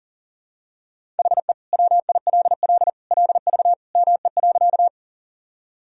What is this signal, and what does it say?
Morse code, heard as a keyed tone, about 37 wpm, 700 Hz. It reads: HE WILL LV ME.
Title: Sentences from Top 400 Words and Abbreviations 40wpm